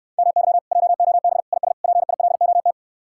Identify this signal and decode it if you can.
Morse code, heard as a keyed tone, about 67 wpm, 700 Hz. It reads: MY JOB IS PILOT